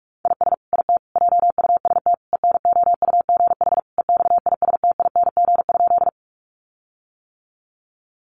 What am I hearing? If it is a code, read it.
Morse code, heard as a keyed tone, about 44 wpm, 700 Hz. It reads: IS IT JUST ENOUGH EXISTING?
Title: Sentences from Top 1500 Words 45wpm